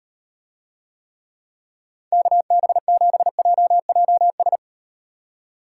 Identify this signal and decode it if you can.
Morse code, heard as a keyed tone, about 38 wpm, 700 Hz. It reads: KB7JJS